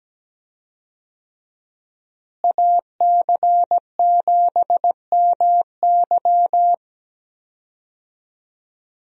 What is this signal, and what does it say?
Morse code, heard as a keyed tone, about 17 wpm, 700 Hz. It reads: AC7MY
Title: USA Call Signs - 5 Characters - Repeated 2x - 17wpm